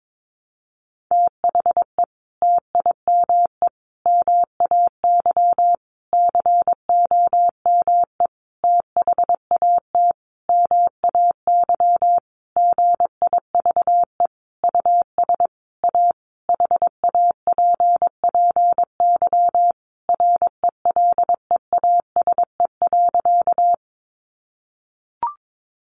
Morse code, heard as a keyed tone, about 22 wpm, 700 Hz. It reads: THE TIME MAY COME THAT MAY GIVE US A HAPPY RELEASE.
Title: Sentences from Top 1100 Words 22wpm